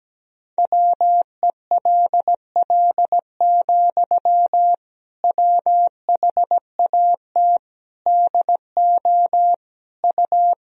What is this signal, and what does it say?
Morse code, heard as a keyed tone, about 17 wpm, 700 Hz. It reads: WELL, WHAT DO U
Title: Sentences from Top 200 Words and Abbreviations 17wpm